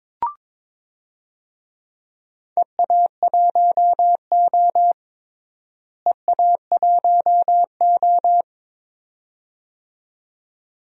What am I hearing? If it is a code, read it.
Morse code, heard as a keyed tone, about 22 wpm, 700 Hz. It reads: EA1O EA1O
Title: DXCC-100 Call Signs - 4 Characters - Repeated 2x - 22wpm